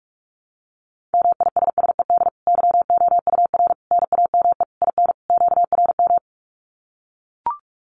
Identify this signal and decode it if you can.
Morse code, heard as a keyed tone, about 44 wpm, 700 Hz. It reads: MISSED YOUR NAME IN QRM
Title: Common QSO Phrases 45wpm